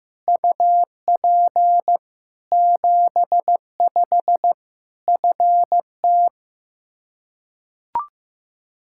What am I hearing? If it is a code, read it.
Morse code, heard as a keyed tone, about 15 wpm, 700 Hz. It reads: UP 75 FT
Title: K1USN SST QSO Practice 15wpm